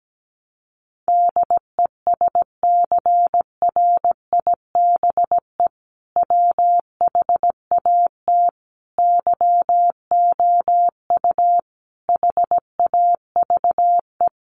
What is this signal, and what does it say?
Morse code, heard as a keyed tone, about 17 wpm, 700 Hz. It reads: DESCRIBE WHAT YOU HAVE
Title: Sentences from Top 1000 Words 17wpm